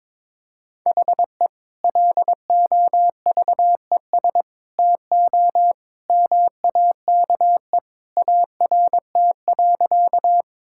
Morse code, heard as a keyed tone, about 22 wpm, 700 Hz. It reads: HE LOVES TO MAKE ART.
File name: practice / 22wpm / Sentences with 2-Word Phrases - 22wpm